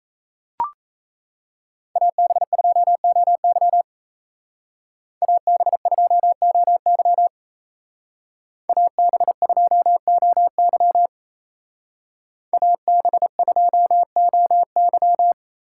Morse code, heard as a keyed tone, about 34 wpm, 700 Hz. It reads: AB2OY AB2OY AB2OY AB2OY
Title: Speed Racing - USA 2x2 Call Signs 28wpm